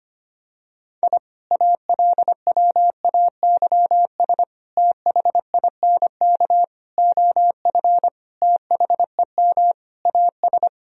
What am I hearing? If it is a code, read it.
Morse code, heard as a keyed tone, about 25 wpm, 700 Hz. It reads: I ALWAYS THINK OF THEM AS
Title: Sentences from Top 1000 Words - Review Part 3 - 25wpm